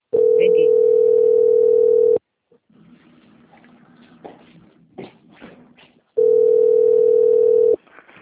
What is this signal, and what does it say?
Ringback tone (North America), 440 + 480 Hz.
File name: out- - - - - .6556